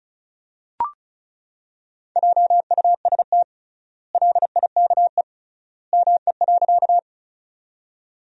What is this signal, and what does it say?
Morse code, heard as a keyed tone, about 34 wpm, 700 Hz. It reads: JUST LIKE ME.
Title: 3x Word-Spacing - Sentences from Top 200 Words - 35wpm